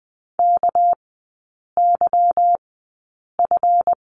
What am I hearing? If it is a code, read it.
Morse code, heard as a keyed tone, about 20 wpm, spaced as for 8 wpm, 690 Hz. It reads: KYF
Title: 198 Lesson 32e - Letter k - Three Characters 20wpm